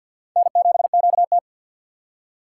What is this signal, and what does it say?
Morse code, heard as a keyed tone, about 53 wpm, 700 Hz. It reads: N7QT
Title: K1USN SST QSO Practice - 50wpm